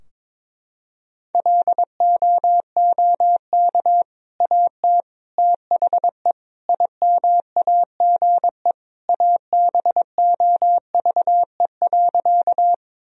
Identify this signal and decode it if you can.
Morse code, heard as a keyed tone, about 22 wpm, 700 Hz. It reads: LOOK AT THE IMAGE ABOVE.
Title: Sentences from Top 1000 Words - Review Part 4 - 22wpm